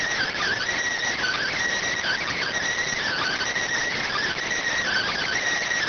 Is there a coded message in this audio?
Yes: Emergency Alert System (SAME) tones (header bursts) from 1.1 to 5.4 s, heard as harsh buzzing bursts.